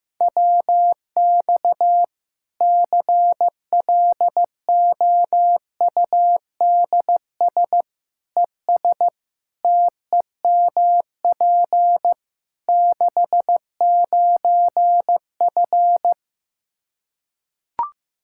Morse code, heard as a keyed tone, about 15 wpm, 700 Hz. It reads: WX CLOUDS ES TEMP 69F
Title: K1USN SST QSO Practice 15wpm